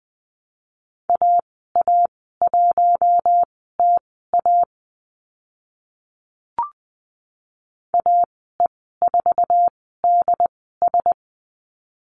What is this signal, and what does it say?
Morse code, heard as a keyed tone, about 20 wpm, spaced as for 14 wpm, 700 Hz. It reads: AA1TA AE4DS
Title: 069 Lesson 12g - Letter d - Call Signs 20wpm